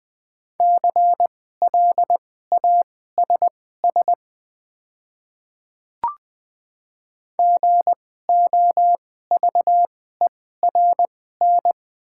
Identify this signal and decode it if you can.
Morse code, heard as a keyed tone, about 20 wpm, spaced as for 14 wpm, 700 Hz. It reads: CLASS GOVERN
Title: 2x Character Spacing - Top 500 Words - 20wpm